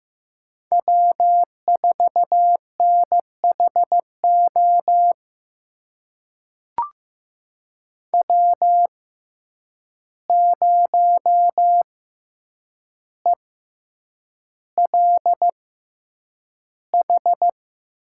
Morse code, heard as a keyed tone, about 15 wpm, spaced as for 8 wpm, 700 Hz. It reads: W4NHO W0ELH
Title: USA Call Signs - 5 Characters - 6x Character Spacing - 15wpm